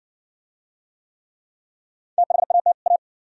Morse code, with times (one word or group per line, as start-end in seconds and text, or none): 2.2-2.7 THAT
2.9-3.0 A